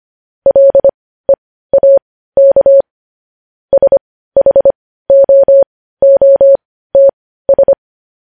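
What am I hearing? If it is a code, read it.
Morse code, heard as a keyed tone, about 25 wpm, spaced as for 15 wpm, 550 Hz. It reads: LEAK SHOOTS
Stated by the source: Morse code news headlines at 15 wpm.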